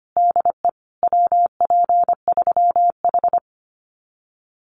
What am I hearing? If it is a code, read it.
Morse code, heard as a keyed tone, about 25 wpm, 700 Hz. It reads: DE WP3H
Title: Common QSO Phrases Encore 25wpm